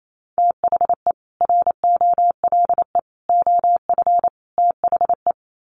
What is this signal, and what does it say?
Morse code, heard as a keyed tone, about 28 wpm, 700 Hz. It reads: THE ROLE OF THE
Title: Sets of 4 Words 28wpm